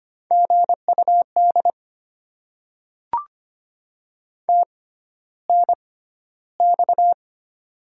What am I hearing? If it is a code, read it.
Morse code, heard as a keyed tone, about 25 wpm, spaced as for 13 wpm, 700 Hz. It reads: GUD TNX
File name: Top 20 QSO Elements - 6x Character Spacing - 25wpm